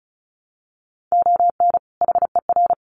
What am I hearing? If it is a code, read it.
Morse code, heard as a keyed tone, about 34 wpm, 700 Hz. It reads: ON HER
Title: Sets of 2 Words 35wpm